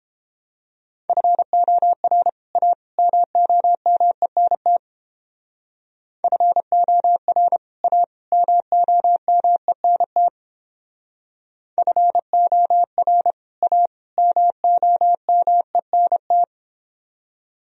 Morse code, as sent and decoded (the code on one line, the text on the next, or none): ..-. --- .-. / .- / -- --- -- . -. - / ..-. --- .-. / .- / -- --- -- . -. - / ..-. --- .-. / .- / -- --- -- . -. -
FOR A MOMENT FOR A MOMENT FOR A MOMENT